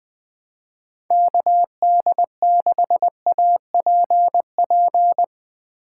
Morse code, heard as a keyed tone, about 20 wpm, 700 Hz. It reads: KD6APP
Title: USA Call Signs - 6 Characters - 6x Character Spacing - 20wpm